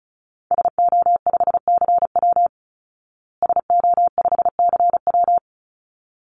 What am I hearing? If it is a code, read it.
Morse code, heard as a keyed tone, about 34 wpm, 700 Hz. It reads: SO5CW SO5CW
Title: Call Signs Repeated 3x - 35wpm